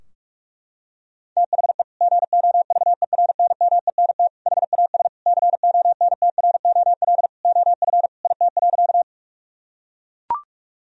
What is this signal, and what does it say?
Morse code, heard as a keyed tone, about 42 wpm, 700 Hz. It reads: THE GOVERNMENT HAS CONTROL OF IT.
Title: Sentences from Top 1000 Words - Review Part 3 - 45wpm